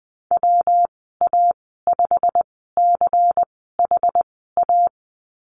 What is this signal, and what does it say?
Morse code, heard as a keyed tone, about 20 wpm, spaced as for 15 wpm, 700 Hz. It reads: WA5CHA
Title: 200 Lesson 32g - Letter k - Call Signs 20wpm